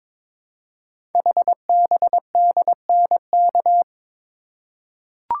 Morse code, heard as a keyed tone, about 22 wpm, 700 Hz. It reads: HBDNK
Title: Code Groups - 5 Letters - 22wpm